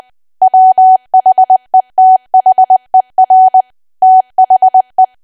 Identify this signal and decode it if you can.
Morse code, heard as a keyed tone, about 20 wpm, 750 Hz. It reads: WHETHER THE
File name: Archive / 20 WPM / 20WPM